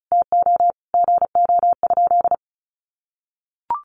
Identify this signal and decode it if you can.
Morse code, heard as a keyed tone, about 34 wpm, 700 Hz. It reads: TO GO?
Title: Sentences from Top 1300 Words 35wpm